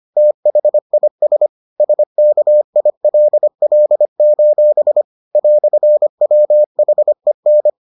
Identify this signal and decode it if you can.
Morse code, heard as a keyed tone, about 25 wpm, 600 Hz. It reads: THIS SKILL: "WHEN